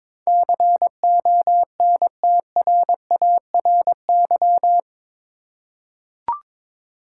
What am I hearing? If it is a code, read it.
Morse code, heard as a keyed tone, about 21 wpm, 700 Hz. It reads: CONTRARY
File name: practice / 22wpm / Eight Letter Words - 22wpm